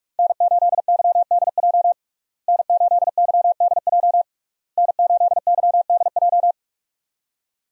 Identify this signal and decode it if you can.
Morse code, heard as a keyed tone, about 46 wpm, 700 Hz. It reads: N8YDJ N8YDJ N8YDJ
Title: USA Call Signs - 5 Characters - Repeated 3x - 45wpm